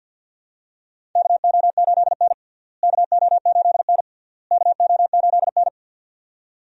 Morse code, heard as a keyed tone, about 50 wpm, 700 Hz. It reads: KO8N KO8N KO8N
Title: USA Call Signs - 4 Characters - Repeated 3x - 50wpm